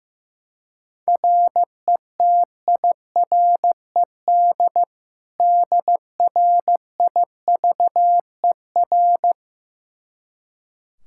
Morse code, heard as a keyed tone, about 15 wpm, 700 Hz. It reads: RETIRED DRIVER
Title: K1USN SST QSO Practice 15wpm